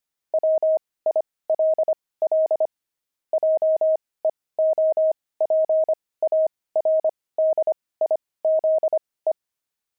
Morse code, heard as a keyed tone, about 25 wpm, spaced as for 18 wpm, 630 Hz. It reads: WILL JEOPARDIZE
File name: pangram-fs-18-cs-25-enemy